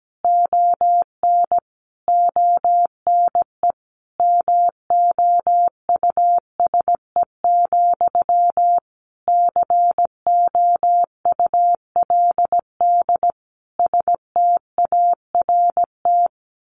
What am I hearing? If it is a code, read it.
Morse code, heard as a keyed tone, about 17 wpm, 700 Hz. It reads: ON ONE MOUSE, COULD START
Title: A Sound Of Thunder 17wpm